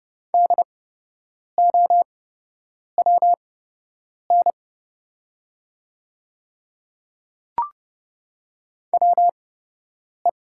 Morse code, heard as a keyed tone, about 30 wpm, spaced as for 8 wpm, 700 Hz. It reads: DOWN WE
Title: 8x Character Spacing - Top 100 Words - 30wpm